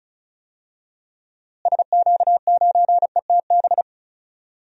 Morse code, heard as a keyed tone, about 34 wpm, 700 Hz. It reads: SQ9ETB
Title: DXCC-100 Call Signs - 6 Characters - Repeated 2x - 35wpm